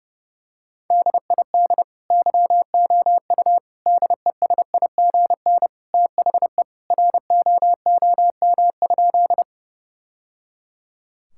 Morse code, heard as a keyed tone, about 32 wpm, 700 Hz. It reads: DID YOU DESIGN THE ROOM?
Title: Sentences from Top 1000 Words - Review Part 5 - 30wpm